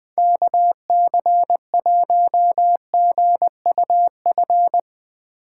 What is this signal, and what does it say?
Morse code, heard as a keyed tone, about 20 wpm, 700 Hz. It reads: KC1GUF
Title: USA Call Signs - 6 Characters - 3x Character Spacing - 20wpm